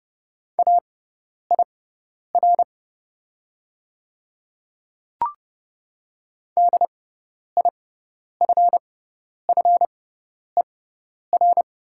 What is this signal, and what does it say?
Morse code, heard as a keyed tone, about 30 wpm, spaced as for 10 wpm, 700 Hz. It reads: AIR DIFFER
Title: 6x Character Spacing - Top 200 Words - 30wpm